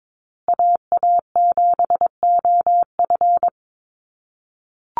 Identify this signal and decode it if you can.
Morse code, heard as a keyed tone, about 22 wpm, 700 Hz. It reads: AA7OF
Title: USA Call Signs Repeated 3x 22wpm